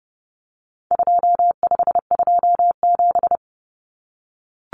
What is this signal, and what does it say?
Morse code, heard as a keyed tone, about 30 wpm, 700 Hz. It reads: 2527
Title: Numbers - 4-Digit 30wpm